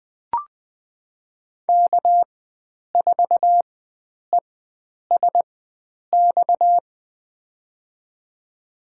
Morse code, heard as a keyed tone, about 20 wpm, spaced as for 9 wpm, 700 Hz. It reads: K4ESX
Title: USA Call Signs - 5 Characters - 4x Character Spacing - 20wpm